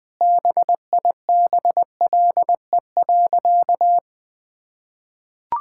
Morse code, heard as a keyed tone, about 21 wpm, 700 Hz. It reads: BIBLE.
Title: Sentences with 9-Word Phrases - Spoken Beginning - 20wpm